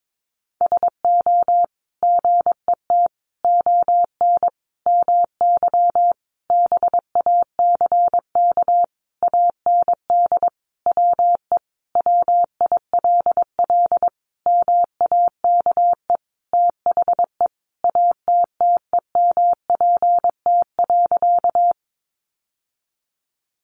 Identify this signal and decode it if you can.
Morse code, heard as a keyed tone, about 22 wpm, 700 Hz. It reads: SO GET ON MY BACK AND WE WILL MAKE THE ATTEMPT.
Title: Sentences from Top 1000 Words 22wpm